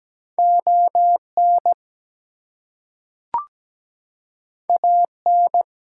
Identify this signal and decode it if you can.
Morse code, heard as a keyed tone, about 17 wpm, 700 Hz. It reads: ON AN
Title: Two Letter Words - 17wpm